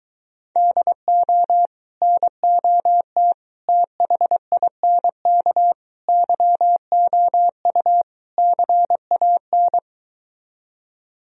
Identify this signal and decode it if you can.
Morse code, heard as a keyed tone, about 23 wpm, 700 Hz. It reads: DO NOT THINK YOU CAN